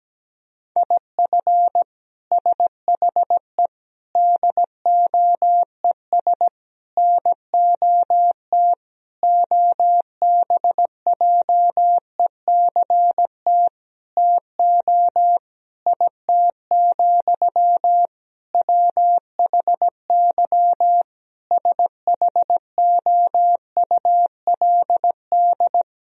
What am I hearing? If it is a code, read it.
Morse code, heard as a keyed tone, about 17 wpm, 700 Hz. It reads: IF SHE DOES NOT OBJECT TO IT, WHY SHOULD
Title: Sentences from Top 1000 Words - Review Part 3 - 17wpm